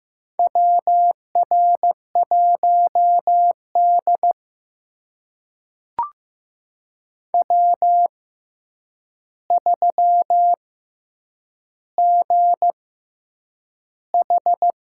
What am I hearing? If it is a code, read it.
Morse code, heard as a keyed tone, about 15 wpm, spaced as for 8 wpm, 700 Hz. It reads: WR1D W3GH
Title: USA Call Signs - 4 Characters - 6x Character Spacing - 15wpm